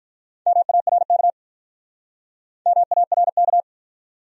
Morse code, heard as a keyed tone, about 48 wpm, 700 Hz. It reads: MARK MARK